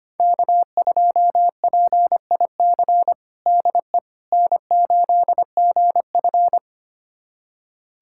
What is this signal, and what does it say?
Morse code, heard as a keyed tone, about 25 wpm, 700 Hz. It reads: K2PIC DE N8GF
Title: USA Callsign DE Callsign 25wpm